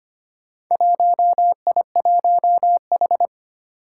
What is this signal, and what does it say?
Morse code, heard as a keyed tone, about 25 wpm, 700 Hz. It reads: 1I1H
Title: Four Challenging Letter-Number Combinations 25wpm